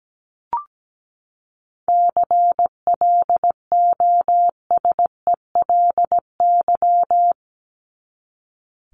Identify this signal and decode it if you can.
Morse code, heard as a keyed tone, about 17 wpm, 700 Hz. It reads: CLOSELY
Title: Top 1251-1500 Words 17wpm